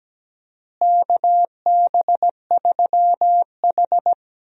Morse code, heard as a keyed tone, about 17 wpm, 700 Hz. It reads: KB3H
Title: USA Call Signs - 4 Characters - Repeated 3x - 17wpm